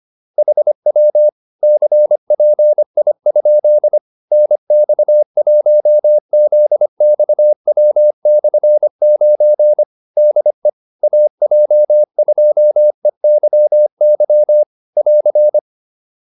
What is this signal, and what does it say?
Morse code, heard as a keyed tone, about 25 wpm, 600 Hz. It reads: HW CPI? NX1ZXW/9 DE AJ2EYY <AR>
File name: QSO307a 25